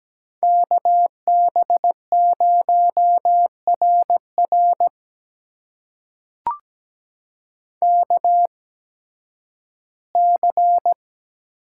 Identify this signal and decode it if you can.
Morse code, heard as a keyed tone, about 17 wpm, spaced as for 11 wpm, 700 Hz. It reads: KB0RR KC
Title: USA Call Signs - 8x Character Spacing - 17wpm